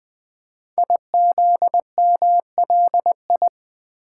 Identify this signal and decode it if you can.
Morse code, heard as a keyed tone, about 20 wpm, 700 Hz. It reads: IZMLI
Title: Code Groups - 5 Letters - 20wpm